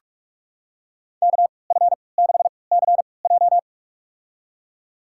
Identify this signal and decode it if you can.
Morse code, heard as a keyed tone, about 46 wpm, spaced as for 26 wpm, 700 Hz. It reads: KF6CJ
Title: USA Call Signs - 3x Character Spacing - 45wpm